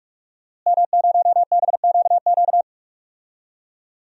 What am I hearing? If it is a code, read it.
Morse code, heard as a keyed tone, about 44 wpm, 700 Hz. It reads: M0BQQ
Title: DXCC-100 Call Signs - 5 Characters - Repeated 2x - 45wpm